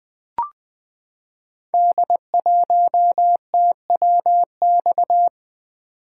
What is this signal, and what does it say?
Morse code, heard as a keyed tone, about 20 wpm, 700 Hz. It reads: D1TWX
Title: Code Groups - 5 Letter:Numbers - 20wpm